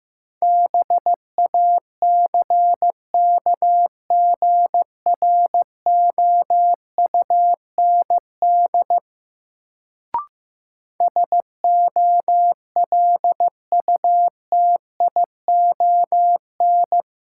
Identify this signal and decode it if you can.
Morse code, heard as a keyed tone, about 15 wpm, 700 Hz. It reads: BACKGROUND SOLUTION
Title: Sentences from Top 1300 Words - New Word Drill - 15wpm